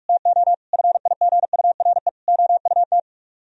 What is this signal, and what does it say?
Morse code, heard as a keyed tone, about 45 wpm, 700 Hz. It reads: TO FIGURE OUT